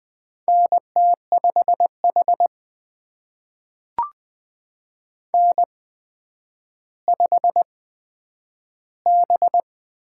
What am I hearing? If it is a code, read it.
Morse code, heard as a keyed tone, about 20 wpm, spaced as for 10 wpm, 700 Hz. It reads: NT5H N5B
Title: USA Call Signs - 4 Characters - 8x Character Spacing - 20wpm